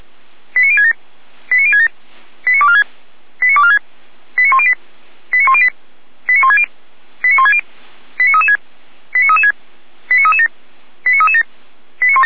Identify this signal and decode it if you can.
Five-tone selective call (ZVEI), heard as quick runs of five tones, tones of about 80 ms. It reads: 89067 89067 89246 89246 89108 89108 89160 89160 89336 89336 89337 89337
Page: /455/SDR/ZVEI/